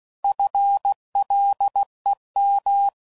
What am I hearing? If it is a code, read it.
Morse code, heard as a keyed tone, about 16 wpm, 800 Hz. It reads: FLEM
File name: Prefixes / By First Letter / Fs